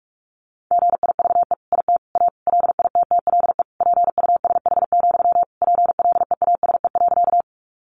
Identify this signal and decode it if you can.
Morse code, heard as a keyed tone, about 44 wpm, 700 Hz. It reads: GIVE IT A LITTLE PUSH, PLEASE.
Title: Sentences from Top 1500 Words 45wpm